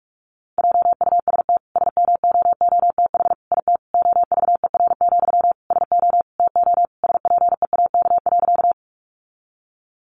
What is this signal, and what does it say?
Morse code, heard as a keyed tone, about 44 wpm, 700 Hz. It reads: JUST SMOOTH IT OVER, SO TO SPEAK.
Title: Sentences from Top 1400 Words 45wpm